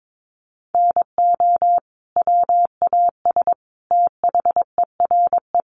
Morse code, heard as a keyed tone, about 22 wpm, 700 Hz. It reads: NO WAS THERE
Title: Sentences from Top 100 Words 22wpm